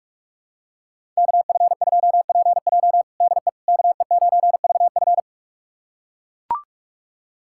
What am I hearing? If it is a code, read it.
Morse code, heard as a keyed tone, about 41 wpm, 700 Hz. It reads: KF2PJ DE KE9VF
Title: USA Callsign DE Callsign - 5 Characters - 45wpm